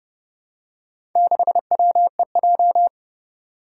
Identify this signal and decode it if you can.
Morse code, heard as a keyed tone, about 30 wpm, 700 Hz. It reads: 6WEJ
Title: Four Challenging Letter-Number Combinations 30wpm